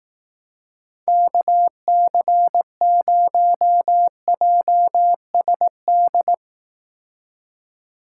Morse code, heard as a keyed tone, about 19 wpm, 700 Hz. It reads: KC0JSD